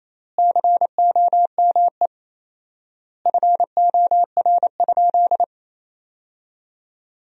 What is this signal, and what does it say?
Morse code, heard as a keyed tone, about 28 wpm, 700 Hz. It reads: COME FOR?
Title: 4x Word-Spacing - Sentences from Top 100 Words - 28wpm